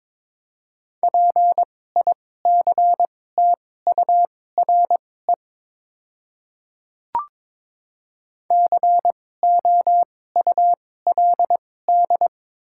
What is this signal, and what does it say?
Morse code, heard as a keyed tone, about 22 wpm, spaced as for 16 wpm, 700 Hz. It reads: PICTURE COULD
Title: 2x Character Spacing - Top 200 Words - 22wpm